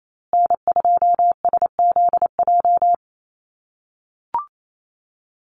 Morse code, heard as a keyed tone, about 28 wpm, 700 Hz. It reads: N2SZJ
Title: USA Call Signs Repeated 2x 28wpm